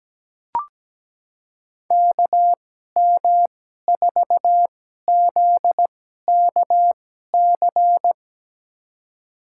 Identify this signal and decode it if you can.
Morse code, heard as a keyed tone, about 17 wpm, spaced as for 12 wpm, 700 Hz. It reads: KM4ZKC
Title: USA Call Signs - 6 Characters - 2x Character Spacing - 17wpm